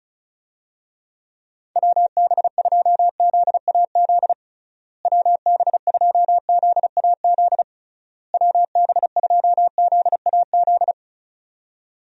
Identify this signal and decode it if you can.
Morse code, heard as a keyed tone, about 34 wpm, 700 Hz. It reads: WB2ZAZ WB2ZAZ WB2ZAZ
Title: USA Call Signs - 6 Characters - Repeated 3x - 35wpm